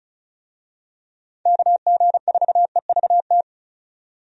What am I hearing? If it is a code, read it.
Morse code, heard as a keyed tone, about 34 wpm, 700 Hz. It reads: KG4EVT
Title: USA Call Signs - 6 Characters - 35wpm